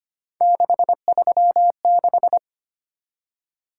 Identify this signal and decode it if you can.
Morse code, heard as a keyed tone, about 25 wpm, 700 Hz. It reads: 636
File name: Three Letter-Number Combinations - 25wpm